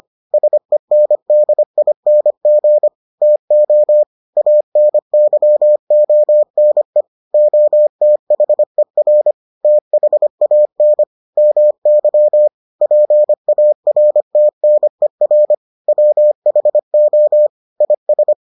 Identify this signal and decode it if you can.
Morse code, heard as a keyed tone, about 25 wpm, 600 Hz. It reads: SENDING TO ANYONE OTHER THAN MY PARTNER WHO IS